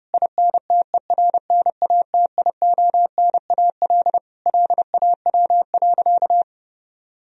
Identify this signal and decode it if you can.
Morse code, heard as a keyed tone, about 32 wpm, 700 Hz. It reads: INTERNATIONAL LAW.
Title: Sentences from Top 1000 Words - Review Part 4 - 30wpm